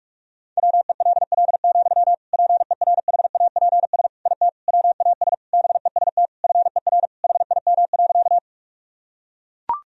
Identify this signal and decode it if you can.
Morse code, heard as a keyed tone, about 41 wpm, 700 Hz. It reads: WELL, PERHAPS IT WAS BEST FER HIM.
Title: Sentences from Top 500 Words and Abbreviations 45wpm